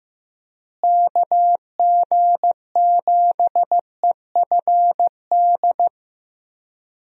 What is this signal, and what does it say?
Morse code, heard as a keyed tone, about 15 wpm, 700 Hz. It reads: KG7EFD
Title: USA Call Signs - 6 Characters - Repeated 3x - 15wpm